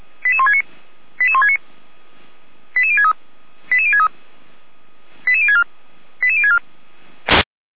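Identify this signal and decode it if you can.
Five-tone selective call (ZVEI), heard as quick runs of five tones, tones of about 70 ms. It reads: 88169 88169 88063 88063 88064 88064
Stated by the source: ZVEI five-tone selective call 88367, received on an SDR.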